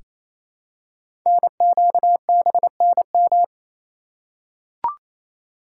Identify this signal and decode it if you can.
Morse code, heard as a keyed tone, about 28 wpm, 700 Hz. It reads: NQBNM
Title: Code Groups - 5 Letters - 28wpm